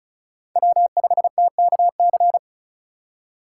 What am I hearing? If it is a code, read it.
Morse code, heard as a keyed tone, about 36 wpm, 700 Hz. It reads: W5TKC
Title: USA Call Signs - 5 Characters - 3x Character Spacing - 35wpm